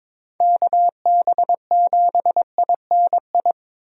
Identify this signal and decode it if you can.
Morse code, heard as a keyed tone, about 22 wpm, 700 Hz. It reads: KB7INI
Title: USA Call Signs - 6 Characters - Repeated 2x - 22wpm